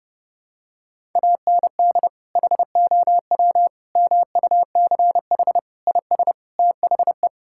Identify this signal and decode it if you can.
Morse code, heard as a keyed tone, about 30 wpm, 700 Hz. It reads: AND HOW MUCH IS THE
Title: Sentences from Top 1000 Words - Review Part 5 - 30wpm